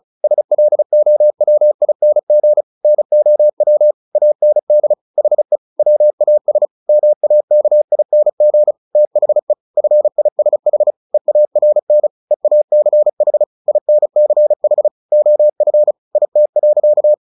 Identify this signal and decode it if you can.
Morse code, heard as a keyed tone, about 36 wpm, 600 Hz. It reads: SLOWING NOW AND HE WAS MAKING THE FISH EARN EACH INCH OF IT.
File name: the old man and the sea 2 35wpm